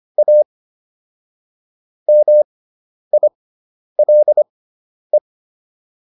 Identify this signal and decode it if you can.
Morse code, heard as a keyed tone, about 25 wpm, spaced as for 10 wpm, 600 Hz. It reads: A MILE